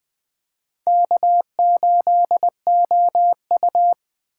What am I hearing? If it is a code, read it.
Morse code, heard as a keyed tone, about 20 wpm, 700 Hz. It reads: K8OU
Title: USA Call Signs - 4 Characters - 4x Character Spacing - 20wpm